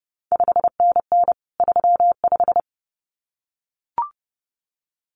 Morse code, heard as a keyed tone, about 30 wpm, 700 Hz. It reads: 5NN 35
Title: CQ WW DX Contest Exchange Encore - 30wpm